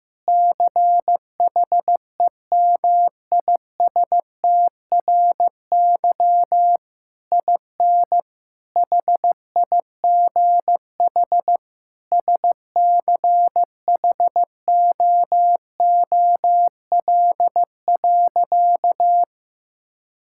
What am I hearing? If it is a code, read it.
Morse code, heard as a keyed tone, about 15 wpm, 700 Hz. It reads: CHEMISTRY IN HIGH SCHOOL.
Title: Sentences from Top 1000 Words - Review Part 6 - 15wpm